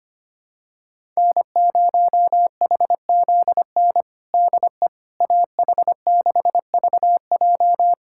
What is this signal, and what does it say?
Morse code, heard as a keyed tone, about 25 wpm, 700 Hz. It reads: N0HZN DE AH6VJ
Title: USA Callsign DE Callsign - 5 Characters - 25wpm